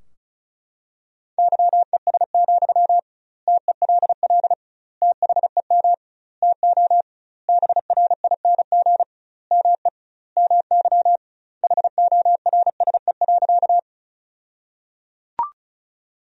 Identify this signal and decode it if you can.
Morse code, heard as a keyed tone, about 34 wpm, 700 Hz. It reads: YES, TELL THEM TO BRING ME MY HORSE.
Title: 2x Word-Spacing - Sentences from Top 500 Words - 35wpm